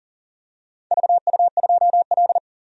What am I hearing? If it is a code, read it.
Morse code, heard as a keyed tone, about 40 wpm, 690 Hz. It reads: VU2L
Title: DXCC-100 Call Signs - 4 Characters - 40wpm